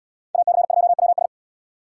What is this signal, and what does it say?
Morse code, heard as a keyed tone, about 77 wpm, 700 Hz. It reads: AC2WS